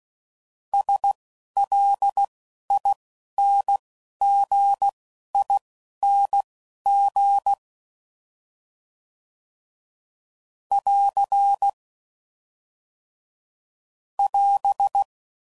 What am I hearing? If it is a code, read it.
Morse code, heard as a keyed tone, about 16 wpm, spaced as for 11 wpm, 800 Hz. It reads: SLINGING <AR> <AS>
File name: Suffixes / Special INGs / Dble INGs